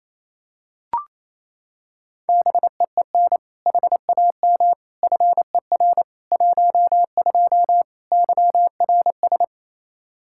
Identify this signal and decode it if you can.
Morse code, heard as a keyed tone, about 28 wpm, 700 Hz. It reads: BEEN HAM FER 12 YRS
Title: Practice QSO - Exchange 3 - More Info - 28wpm